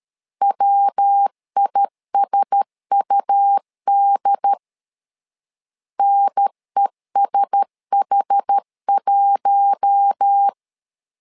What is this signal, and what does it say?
Morse code, heard as a keyed tone, about 13 wpm, 800 Hz. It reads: WISUD NESH1
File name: LEZIONI / 4 Q1S3 12wpm 2023